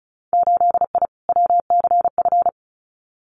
Morse code, heard as a keyed tone, about 36 wpm, 700 Hz. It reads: 8I WCF
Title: Winter Field Day Exchanges Encore 35wpm